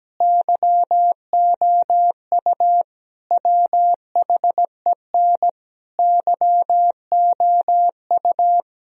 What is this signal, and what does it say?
Morse code, heard as a keyed tone, about 17 wpm, 700 Hz. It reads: YOU WHEN YOU
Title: Sentences from Top 1000 Words - Review Part 4 - 17wpm